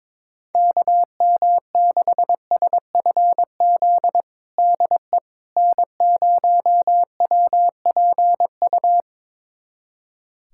Morse code, heard as a keyed tone, about 22 wpm, 700 Hz. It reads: KM6SFZ DE N0WPU
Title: USA Callsign DE Callsign 22wpm